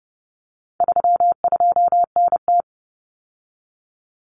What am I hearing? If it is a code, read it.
Morse code, heard as a keyed tone, about 30 wpm, 700 Hz. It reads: 32NT
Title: Common Cut Numbers - 4-Digit 30wpm